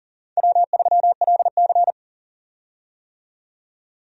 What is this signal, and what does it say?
Morse code, heard as a keyed tone, about 40 wpm, 700 Hz. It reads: W3LC
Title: USA Call Signs - 4 Characters - 40wpm